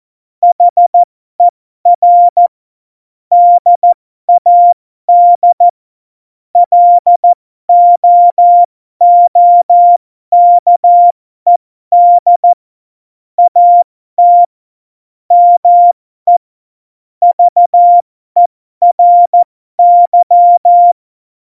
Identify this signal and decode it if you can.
Morse code, heard as a keyed tone, about 14 wpm, spaced as for 12 wpm, 700 Hz. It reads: HER DAD LOOKED AT ME VERY
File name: Morsekurs / Witze